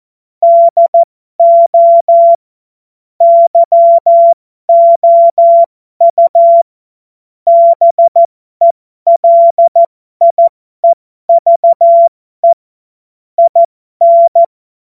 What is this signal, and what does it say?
Morse code, heard as a keyed tone, about 14 wpm, spaced as for 12 wpm, 680 Hz. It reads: DO YOU BELIEVE IN
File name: Morsekurs / Witze